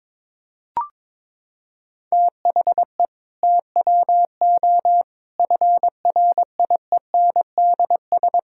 Morse code, heard as a keyed tone, about 22 wpm, 700 Hz. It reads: THE TWO FRIENDS
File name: practice / 22wpm / Sentences with 2-Word Phrases - 22wpm